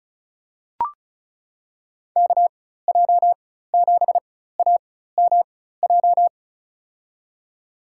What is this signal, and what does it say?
Morse code, heard as a keyed tone, about 34 wpm, spaced as for 16 wpm, 700 Hz. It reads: KJ7AMJ
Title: USA Call Signs - 6 Characters - 4x Character Spacing - 35wpm